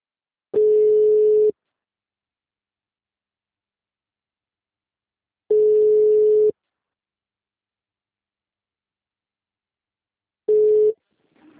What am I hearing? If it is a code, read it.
Ringback tone (Europe, 425 Hz).